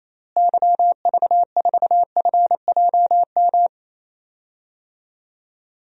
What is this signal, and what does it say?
Morse code, heard as a keyed tone, about 28 wpm, 700 Hz. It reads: YV4FJM
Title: DXCC-100 Call Signs - 6 Characters - Repeated 2x - 28wpm